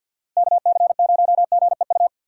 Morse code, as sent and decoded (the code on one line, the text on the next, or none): -.- -.-. ----- --. ..-
KC0GEU